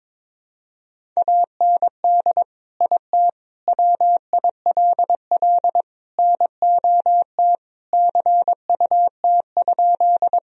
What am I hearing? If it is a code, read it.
Morse code, heard as a keyed tone, about 22 wpm, 700 Hz. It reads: AND IT WILL NOT CUT?
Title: Sentences from Top 1000 Words - Review Part 1 - 22wpm